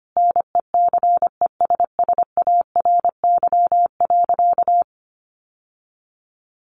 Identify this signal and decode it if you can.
Morse code, heard as a keyed tone, about 25 wpm, 700 Hz. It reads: NECESSARY.
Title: Sentences from Top 800 Words 25wpm